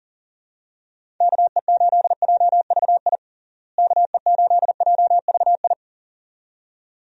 Morse code, heard as a keyed tone, about 37 wpm, 700 Hz. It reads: KE8JVI KE8JVI
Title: USA Call Signs - 6 Characters - Repeated 2x - 40wpm